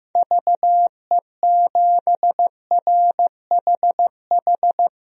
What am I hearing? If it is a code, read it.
Morse code, heard as a keyed tone, about 15 wpm, 700 Hz. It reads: VE7RHH
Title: DXCC-100 Call Signs - 6 Characters - 15wpm